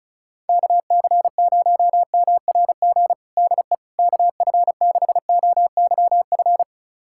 Morse code, heard as a keyed tone, about 36 wpm, 700 Hz. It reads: KC0MRG DE KF6OYF
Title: USA Callsign DE Callsign - 6 Characters - 35wpm